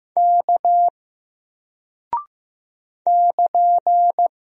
Morse code, heard as a keyed tone, about 15 wpm, 700 Hz. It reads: K <KN>
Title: Single Character (Letter:Number:Punctuation:Prosign) Alt 15wpm